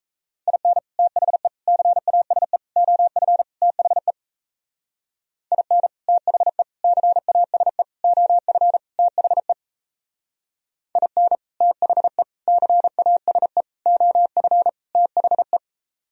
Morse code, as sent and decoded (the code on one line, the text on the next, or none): .. -. / - .... / -.-. .- ... / --- ..-. / - .... / .. -. / - .... / -.-. .- ... / --- ..-. / - .... / .. -. / - .... / -.-. .- ... / --- ..-. / - .... .
IN THE CASE OF THE IN THE CASE OF THE IN THE CASE OF THE